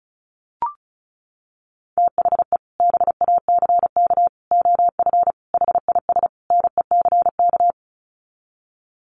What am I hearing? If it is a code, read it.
Morse code, heard as a keyed tone, about 34 wpm, 700 Hz. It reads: THE BACK OF HIS NECK
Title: Sets of 5 Words Encore 35wpm